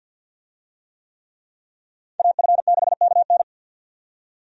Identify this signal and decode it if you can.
Morse code, heard as a keyed tone, about 50 wpm, 700 Hz. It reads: AF6KN
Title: USA Call Signs - 5 Characters - 50wpm